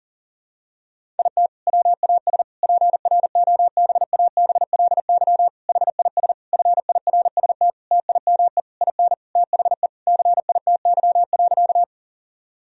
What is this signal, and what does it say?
Morse code, heard as a keyed tone, about 37 wpm, 700 Hz. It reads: IT WAS PROBABLY HIS FIRST TIME IN THE CITY.
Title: Sentences from Top 1000 Words - Review Part 4 - 40wpm